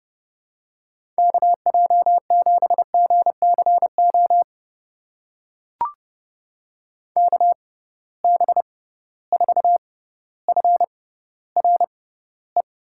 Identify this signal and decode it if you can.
Morse code, heard as a keyed tone, about 31 wpm, spaced as for 15 wpm, 700 Hz. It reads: KJ7GCO KB4FRE
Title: USA Call Signs - 6x Character Spacing - 30wpm